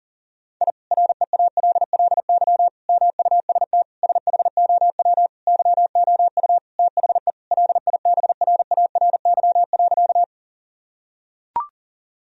Morse code, heard as a keyed tone, about 40 wpm, 700 Hz. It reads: I REALLY MUST SHOW YOU THE LIBRARY.
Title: Sentences from Top 1000 Words - Review Part 4 - 40wpm